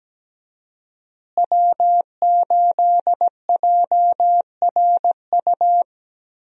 Morse code, heard as a keyed tone, about 17 wpm, 700 Hz. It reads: W8JRU
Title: USA Call Signs - 5 Characters - Repeated 3x - 17wpm